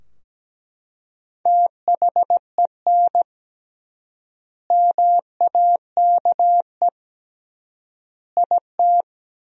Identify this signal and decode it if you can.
Morse code, heard as a keyed tone, about 17 wpm, 700 Hz. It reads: THEN MAKE IT
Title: 3x Word-Spacing - Sentences from Top 100 Words - 17wpm